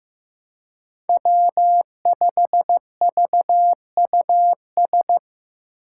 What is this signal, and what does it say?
Morse code, heard as a keyed tone, about 15 wpm, 700 Hz. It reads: W5VUS
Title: Five Challenging Letter-Number Combinations 15wpm